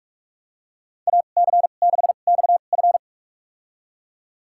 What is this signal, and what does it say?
Morse code, heard as a keyed tone, about 46 wpm, spaced as for 32 wpm, 700 Hz. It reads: AC6XF
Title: USA Call Signs - 5 Characters - 2x Character Spacing - 45wpm